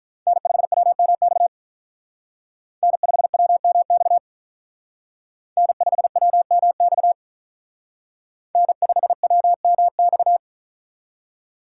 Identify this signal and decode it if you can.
Morse code, heard as a keyed tone, about 43 wpm, 700 Hz. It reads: N5WMX N5WMX N5WMX N5WMX
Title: Speed Racing - USA 1x3 Call Signs 35wpm